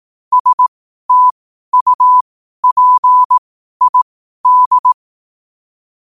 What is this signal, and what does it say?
Morse code, heard as a keyed tone, about 18 wpm, spaced as for 12 wpm, 1000 Hz. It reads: STUPID
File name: qotdfiles / 12wpm